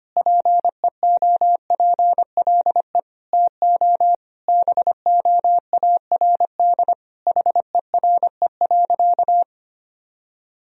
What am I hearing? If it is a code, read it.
Morse code, heard as a keyed tone, about 25 wpm, 700 Hz. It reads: PEOPLE TO BOARD HERE.
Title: Sentences from Top 1000 Words - Review Part 4 - 25wpm